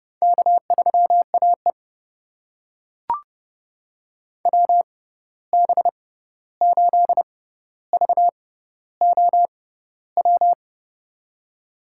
Morse code, heard as a keyed tone, about 31 wpm, spaced as for 14 wpm, 700 Hz. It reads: K3AE WB8VOW
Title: USA Call Signs - 6x Character Spacing - 30wpm